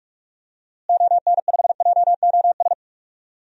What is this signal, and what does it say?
Morse code, heard as a keyed tone, about 46 wpm, 700 Hz. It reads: ON5JOS
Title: DXCC-100 Call Signs - 6 Characters - Repeated 2x - 45wpm